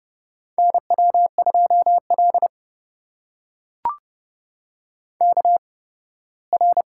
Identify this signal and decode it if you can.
Morse code, heard as a keyed tone, about 30 wpm, spaced as for 18 wpm, 700 Hz. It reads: NW2L KR
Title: USA Call Signs - 4 Characters - 8x Character Spacing - 30wpm